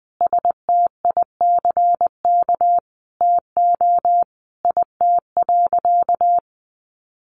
Morse code, heard as a keyed tone, about 20 wpm, 700 Hz. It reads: STICK TO IT.
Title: Sentences from Top 1200 Words 20wpm